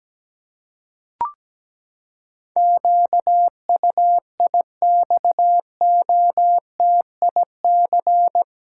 Morse code, heard as a keyed tone, about 17 wpm, 700 Hz. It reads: QUIXOTIC
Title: Pangram Challenge v2 - 17wpm